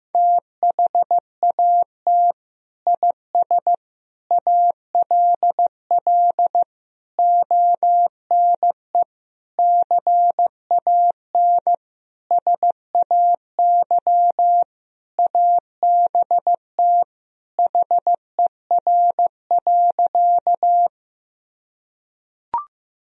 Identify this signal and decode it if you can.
Morse code, heard as a keyed tone, about 16 wpm, 700 Hz. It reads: THAT IS ALL ONE CAN SAY ABT HER.
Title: Sentences from Top 200 Words and Abbreviations 15wpm